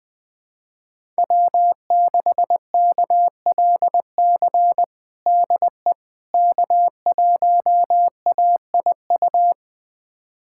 Morse code, heard as a keyed tone, about 20 wpm, 700 Hz. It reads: W6KLC DE K1AIU
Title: USA Callsign DE Callsign - 5 Characters - 20wpm